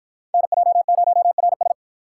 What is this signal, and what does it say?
Morse code, heard as a keyed tone, about 49 wpm, 700 Hz. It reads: NJ0RS